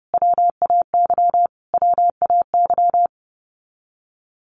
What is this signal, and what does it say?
Morse code, heard as a keyed tone, about 28 wpm, 700 Hz. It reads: WAY WAY
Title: Top 300 Words Repeated 30wpm